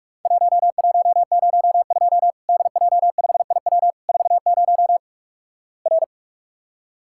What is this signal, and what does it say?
Morse code, heard as a keyed tone, about 46 wpm, 700 Hz. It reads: 1102 DJ5IW 40 R
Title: Worked All Europe DX Contest - QTC Exchange - 45wpm